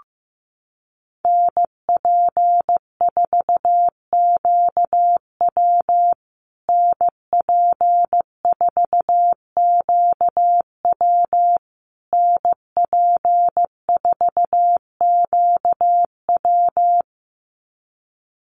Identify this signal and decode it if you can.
Morse code, heard as a keyed tone, about 15 wpm, 700 Hz. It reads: NP4QW NP4QW NP4QW
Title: USA Call Signs Repeated 3x 15wpm